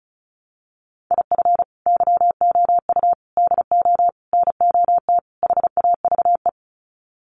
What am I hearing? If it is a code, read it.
Morse code, heard as a keyed tone, about 34 wpm, 700 Hz. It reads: IF YOU DO NOT HAVE
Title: Sets of 5 Words 35wpm